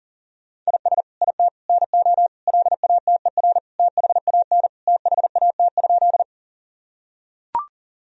Morse code, heard as a keyed tone, about 40 wpm, 700 Hz. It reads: IS IT NO LATER THAN THAT?
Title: Sentences from Top 1000 Words - Review Part 4 - 40wpm